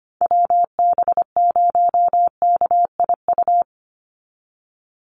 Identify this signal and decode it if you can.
Morse code, heard as a keyed tone, about 25 wpm, 700 Hz. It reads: WB0KIU
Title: NA SOTA Chaser Callsigns 25wpm